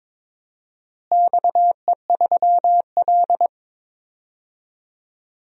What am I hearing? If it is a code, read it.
Morse code, heard as a keyed tone, about 22 wpm, 700 Hz. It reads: XE3L
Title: DXCC-100 Call Signs - 4 Characters - Repeated 3x - 22wpm